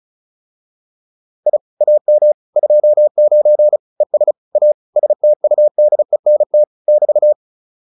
Morse code, heard as a keyed tone, about 35 wpm, 600 Hz. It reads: I AM 29 ES A STUDENT <BT>